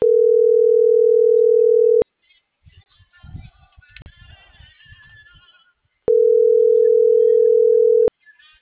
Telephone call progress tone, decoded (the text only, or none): ringback tone